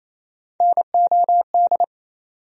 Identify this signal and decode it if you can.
Morse code, heard as a keyed tone, about 28 wpm, 700 Hz. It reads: NOD
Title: USA Call Sign Postfixes - 3 Letters 28wpm